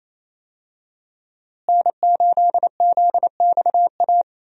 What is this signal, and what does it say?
Morse code, heard as a keyed tone, about 28 wpm, 700 Hz. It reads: N8ZXA
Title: USA Call Signs - 5 Characters - Repeated 3x - 28wpm